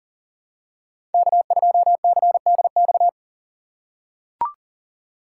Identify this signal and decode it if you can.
Morse code, heard as a keyed tone, about 40 wpm, 700 Hz. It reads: K2CDX
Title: K1USN SST QSO Practice - 40wpm